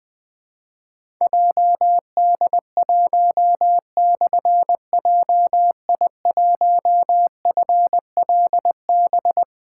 Morse code, heard as a keyed tone, about 20 wpm, 700 Hz. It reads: JD1/JI1FLB
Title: Difficult Call Signs 20wpm